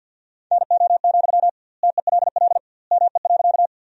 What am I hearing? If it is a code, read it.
Morse code, heard as a keyed tone, about 50 wpm, 700 Hz. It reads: NO, TELL ME.
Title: 2x Word-Spacing - Sentences from Top 200 Words - 50wpm